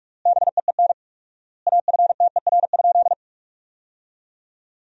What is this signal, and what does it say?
Morse code, heard as a keyed tone, about 47 wpm, 700 Hz. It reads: BEEN AFTER?
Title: 4x Word-Spacing - Sentences from Top 200 Words - 45wpm